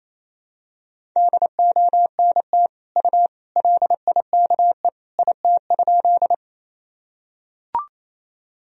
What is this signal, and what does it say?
Morse code, heard as a keyed tone, about 28 wpm, 700 Hz. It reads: DONT U LIKE IT?
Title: Sentences from Top 300 Words and Abbreviations 28wpm